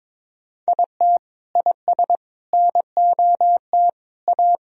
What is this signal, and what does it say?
Morse code, heard as a keyed tone, about 22 wpm, 700 Hz. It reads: IT IS NOT A
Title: Sentences from Top 1000 Words - Review Part 4 - 22wpm